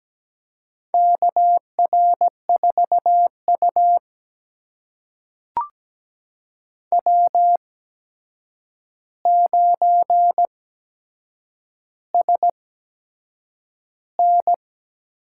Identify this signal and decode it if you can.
Morse code, heard as a keyed tone, about 17 wpm, spaced as for 7 wpm, 700 Hz. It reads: KR4U W9SN
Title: USA Call Signs - 4 Characters - 8x Character Spacing - 17wpm